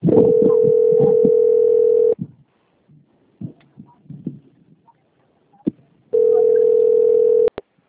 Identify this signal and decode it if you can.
Ringback tone (North America), 440 + 480 Hz.